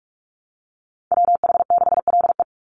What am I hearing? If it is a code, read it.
Morse code, heard as a keyed tone, about 44 wpm, 700 Hz. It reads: WH6LE
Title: NA SOTA Activator Callsigns 45wpm